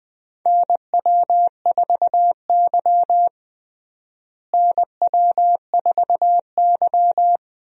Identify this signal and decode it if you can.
Morse code, heard as a keyed tone, about 21 wpm, 700 Hz. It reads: NW4Y NW4Y
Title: USA Call Signs - 4 Characters - Repeated 2x - 20wpm